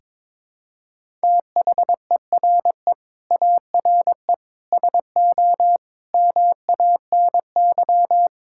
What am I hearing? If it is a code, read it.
Morse code, heard as a keyed tone, about 22 wpm, 700 Hz. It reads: THERE ARE SO MANY
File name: practice / 22wpm / Sentences with 4-Word Phrases - 22wpm